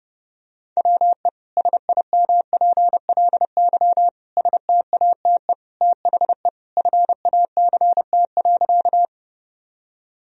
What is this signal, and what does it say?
Morse code, heard as a keyed tone, about 31 wpm, 700 Hz. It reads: WE SIMPLY STATE THE FACT.
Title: Sentences from Top 1000 Words - Review Part 4 - 30wpm